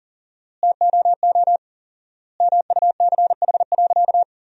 Morse code, heard as a keyed tone, about 37 wpm, 700 Hz. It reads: TOO MUCH.
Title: 4x Word-Spacing - Sentences from Top 400 Words - 40wpm